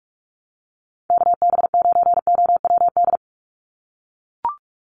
Morse code, heard as a keyed tone, about 44 wpm, 700 Hz. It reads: KB9OWD
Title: Call Signs - 45wpm